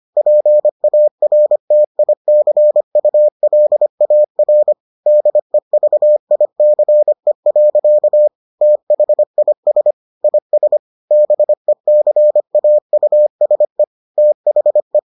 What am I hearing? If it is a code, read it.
Morse code, heard as a keyed tone, about 25 wpm, 600 Hz. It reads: PARTICULAR DEVICE. THIS IS BECAUSE THE